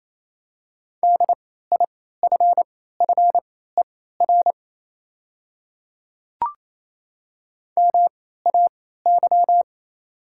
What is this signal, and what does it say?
Morse code, heard as a keyed tone, about 28 wpm, spaced as for 16 wpm, 700 Hz. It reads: DIFFER MAY
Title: 3x Character Spacing - Top 300 Words - 28wpm